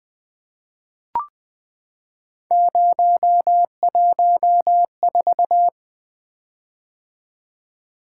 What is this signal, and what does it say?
Morse code, heard as a keyed tone, about 20 wpm, 700 Hz. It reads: 014
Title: Three Letter-Number Combinations - 20wpm